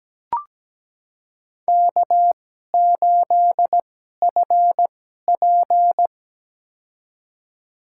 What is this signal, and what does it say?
Morse code, heard as a keyed tone, about 17 wpm, spaced as for 12 wpm, 700 Hz. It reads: K8FP
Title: USA Call Signs - 4 Characters - 2x Character Spacing - 17wpm